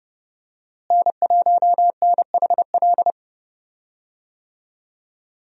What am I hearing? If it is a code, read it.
Morse code, heard as a keyed tone, about 30 wpm, 700 Hz. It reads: N1NHL
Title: USA Call Signs - 5 Characters - Repeated 2x - 30wpm